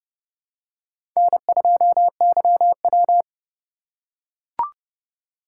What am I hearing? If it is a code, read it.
Morse code, heard as a keyed tone, about 30 wpm, 700 Hz. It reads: N2YW
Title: USA Call Signs - 4 Characters - 2x Character Spacing - 30wpm